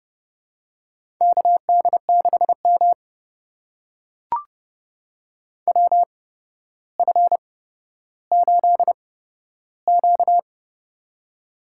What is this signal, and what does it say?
Morse code, heard as a keyed tone, about 30 wpm, spaced as for 13 wpm, 700 Hz. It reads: KD6M WF8Q
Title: USA Call Signs - 4 Characters - 8x Character Spacing - 30wpm